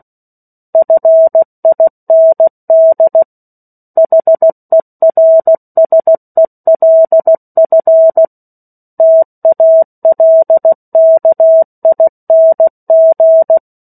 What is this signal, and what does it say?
Morse code, heard as a keyed tone, about 16 wpm, 650 Hz. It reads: FIND HERSELF TALKING
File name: 03-Chapter-03 16 WPM